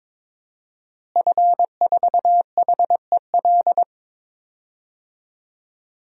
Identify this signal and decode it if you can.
Morse code, heard as a keyed tone, about 21 wpm, 700 Hz. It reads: F4HEL
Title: DXCC-100 Call Signs - 5 Characters - Repeated 3x - 22wpm